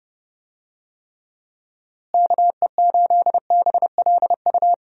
Morse code, heard as a keyed tone, about 30 wpm, 700 Hz. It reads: KE8BLU
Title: USA Call Signs - 6 Characters - Repeated 3x - 30wpm